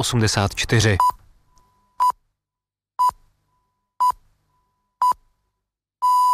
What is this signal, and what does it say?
Radio time signal (time pips), with speech. It is heard from 1.0 to 5.9 s.